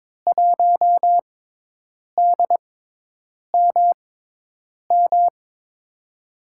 Morse code, heard as a keyed tone, about 22 wpm, spaced as for 8 wpm, 700 Hz. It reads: 1DMM
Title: USA Call Signs - 6 Characters - 6x Character Spacing - 22wpm